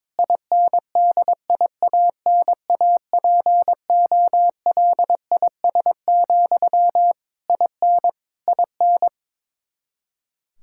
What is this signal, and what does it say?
Morse code, heard as a keyed tone, about 21 wpm, 700 Hz. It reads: INDIANAPOLIS, IN IN
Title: Practice QSO - Exchange 1 - Beginning - 22wpm